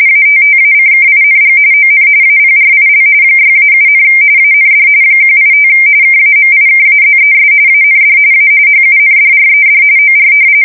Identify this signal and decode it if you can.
RTTY, about 45.45 baud. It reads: ATTACKING ON APRIL 6-7, AND 185 KAMIKAZES AND A HUNDRED OR
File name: fsk170 / battleshipnj